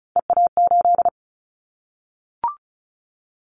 Morse code, heard as a keyed tone, about 34 wpm, 700 Hz. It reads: EA8
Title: Callsign Prefixes for Top 100 DXCC - 35wpm